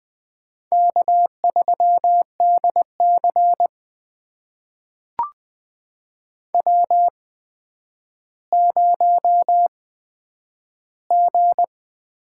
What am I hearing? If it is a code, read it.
Morse code, heard as a keyed tone, about 20 wpm, spaced as for 10 wpm, 700 Hz. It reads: K3DC W0G
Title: USA Call Signs - 4 Characters - 8x Character Spacing - 20wpm